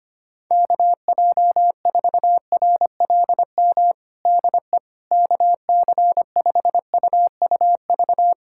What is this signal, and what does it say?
Morse code, heard as a keyed tone, about 25 wpm, 700 Hz. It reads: KJ4RLM DE KC5UUV
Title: USA Callsign DE Callsign - 6 Characters - 25wpm